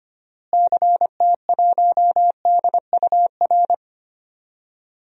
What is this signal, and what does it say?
Morse code, heard as a keyed tone, about 25 wpm, 700 Hz. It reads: CT1DUR
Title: DXCC-100 Call Signs - 6 Characters - Repeated 3x - 25wpm